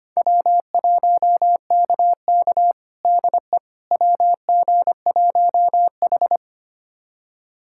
Morse code, heard as a keyed tone, about 25 wpm, 700 Hz. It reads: W1KK DE WG1H
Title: USA Callsign DE Callsign - 4 Characters - 25wpm